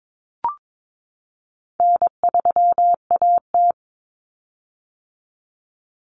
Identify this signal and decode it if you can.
Morse code, heard as a keyed tone, about 22 wpm, 700 Hz. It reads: N3AT
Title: Common Cut Numbers - 4-Digit 22wpm